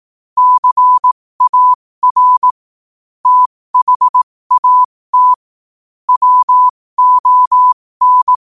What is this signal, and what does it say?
Morse code, heard as a keyed tone, about 18 wpm, spaced as for 15 wpm, 1000 Hz. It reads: CAR THAT WON
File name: qotdfiles / 15wpm